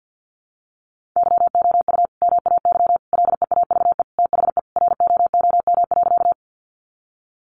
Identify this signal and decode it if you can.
Morse code, heard as a keyed tone, about 50 wpm, 700 Hz. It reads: YOU MAY LEAVE THE ROOM.
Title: Sentences from Top 400 Words 50wpm